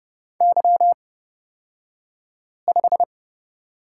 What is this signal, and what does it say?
Morse code, heard as a keyed tone, about 29 wpm, spaced as for 5 wpm, 700 Hz. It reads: Y5
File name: Lesson 38d - Letter q - Two Characters 30wpm